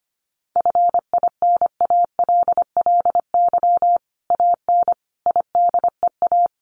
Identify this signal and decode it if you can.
Morse code, heard as a keyed tone, about 25 wpm, 700 Hz. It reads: FINALLY AN IDEA
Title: Sentences from Top 1000 Words 25wpm